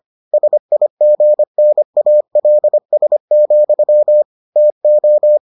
Morse code, heard as a keyed tone, about 25 wpm, 600 Hz. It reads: SIGNALS, TO